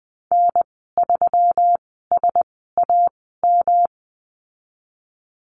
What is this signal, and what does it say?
Morse code, heard as a keyed tone, about 20 wpm, spaced as for 15 wpm, 700 Hz. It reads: N3SAM
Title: 122 Lesson 20g - Number 3 - Call Signs 20wpm